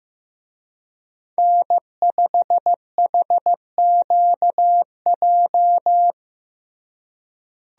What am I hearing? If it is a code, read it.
Morse code, heard as a keyed tone, about 15 wpm, 700 Hz. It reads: N5HQJ
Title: USA Call Signs - 5 Characters - Repeated 3x - 15wpm